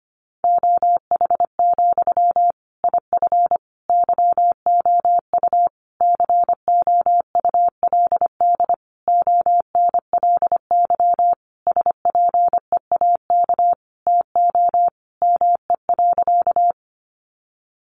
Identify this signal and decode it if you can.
Morse code, heard as a keyed tone, about 25 wpm, 700 Hz. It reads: OH, IF YOU COULD ONLY SPEAK TO ME.
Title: Sentences from Top 1100 Words 25wpm